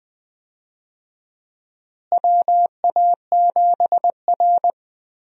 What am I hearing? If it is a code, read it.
Morse code, heard as a keyed tone, about 20 wpm, 700 Hz. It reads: WA7R
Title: USA Call Signs - 4 Characters - Repeated 2x - 20wpm